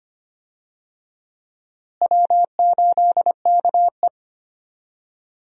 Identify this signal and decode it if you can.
Morse code, heard as a keyed tone, about 25 wpm, 700 Hz. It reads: W8KE
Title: USA Call Signs - 4 Characters - Repeated 3x - 25wpm